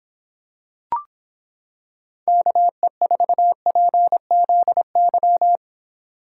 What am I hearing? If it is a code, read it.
Morse code, heard as a keyed tone, about 26 wpm, 700 Hz. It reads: KE4PZY